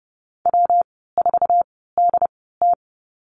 Morse code, heard as a keyed tone, about 30 wpm, spaced as for 17 wpm, 700 Hz. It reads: W4DT
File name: Lesson 19g - Letter w - Call Signs 30wpm